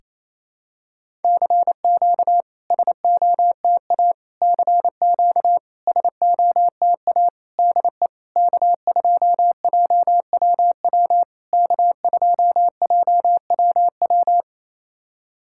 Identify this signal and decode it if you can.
Morse code, heard as a keyed tone, about 28 wpm, 690 Hz. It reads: CQ SOTA CQ SOTA DE K2JWW K2JWW
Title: SOTA QSO Practice 28wpm